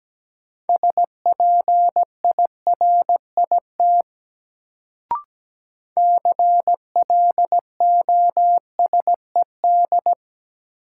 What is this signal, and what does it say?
Morse code, heard as a keyed tone, about 17 wpm, 700 Hz. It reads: SPIRIT CLOSED
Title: Sentences from Top 1400 Words - New Word Drill Alt - 17wpm